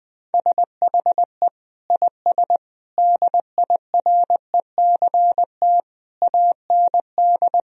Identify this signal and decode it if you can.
Morse code, heard as a keyed tone, about 20 wpm, 700 Hz. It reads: SHE IS DIRECT AND
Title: Sentences from Top 1000 Words - Review Part 4 - 20wpm